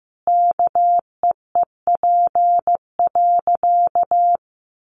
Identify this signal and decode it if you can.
Morse code, heard as a keyed tone, about 15 wpm, 700 Hz. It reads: KEEP.
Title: Sentences from Top 1100 Words 15wpm